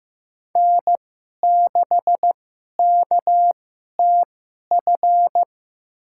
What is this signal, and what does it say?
Morse code, heard as a keyed tone, about 16 wpm, spaced as for 11 wpm, 700 Hz. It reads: N6KTF
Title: USA Call Signs - 5 Characters - 2x Character Spacing - 15wpm